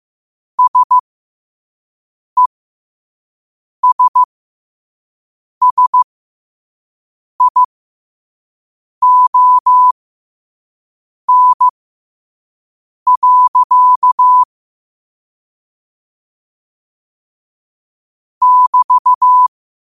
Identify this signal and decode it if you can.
Morse code, heard as a keyed tone, about 15 wpm, spaced as for 5 wpm, 1000 Hz. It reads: SESSION. <BT>
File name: 5wpm